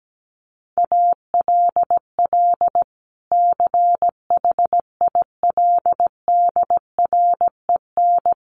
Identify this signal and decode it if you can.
Morse code, heard as a keyed tone, about 17 wpm, 700 Hz. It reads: ALL CHILDREN
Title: Sentences from Top 1200 Words 17wpm